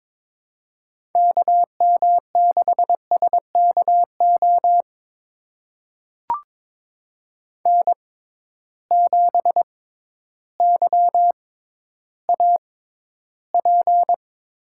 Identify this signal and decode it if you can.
Morse code, heard as a keyed tone, about 22 wpm, spaced as for 12 wpm, 700 Hz. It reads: KM6SKO N7YAP
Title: USA Call Signs - 6x Character Spacing - 22wpm